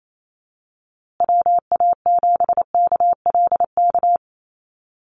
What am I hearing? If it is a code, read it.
Morse code, heard as a keyed tone, about 28 wpm, 700 Hz. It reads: WA7KLK
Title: USA Call Signs Repeated 3x 28wpm